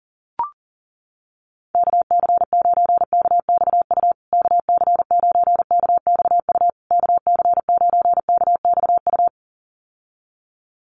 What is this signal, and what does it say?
Morse code, heard as a keyed tone, about 40 wpm, 700 Hz. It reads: KC9KXU KC9KXU KC9KXU
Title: USA Call Signs Repeated 3x 40wpm